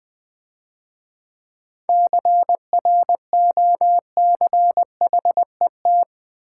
Morse code, heard as a keyed tone, about 24 wpm, 700 Hz. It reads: CROCHET
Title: Difficult Words - Speed-Racing with Hints - 20wpm